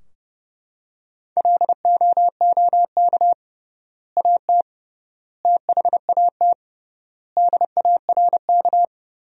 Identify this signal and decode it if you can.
Morse code, heard as a keyed tone, about 30 wpm, 700 Hz. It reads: LOOK AT THAT DARK
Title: 3x Word-Spacing - Sentences from Top 500 Words - 30wpm